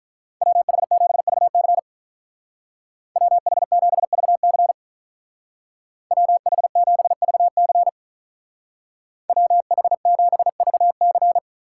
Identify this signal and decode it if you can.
Morse code, heard as a keyed tone, about 45 wpm, 700 Hz. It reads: WH7VC WH7VC WH7VC WH7VC